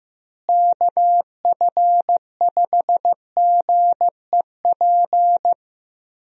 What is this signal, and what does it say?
Morse code, heard as a keyed tone, about 15 wpm, 700 Hz. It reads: KF5GEP
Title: USA Call Signs - 6 Characters - Repeated 2x - 15wpm